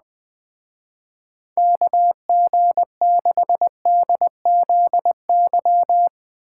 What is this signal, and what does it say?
Morse code, heard as a keyed tone, about 20 wpm, 700 Hz. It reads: KG6DZY
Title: Speed Racing - USA 2x3 Call Signs 17wpm